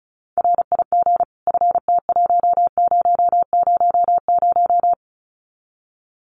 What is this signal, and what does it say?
Morse code, heard as a keyed tone, about 34 wpm, 700 Hz. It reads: RIG FT1000
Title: Common QSO Phrases 35wpm